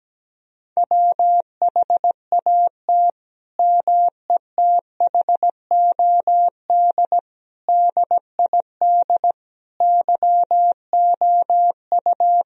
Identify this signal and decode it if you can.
Morse code, heard as a keyed tone, about 17 wpm, 700 Hz. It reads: WHAT METHOD DID YOU
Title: Sentences from Top 1000 Words - Review Part 3 - 17wpm